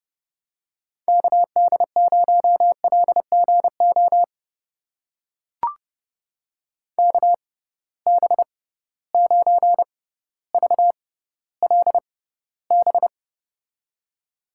Morse code, heard as a keyed tone, about 30 wpm, spaced as for 15 wpm, 700 Hz. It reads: KD0LGO KB9VLB
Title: USA Call Signs - 6 Characters - 6x Character Spacing - 30wpm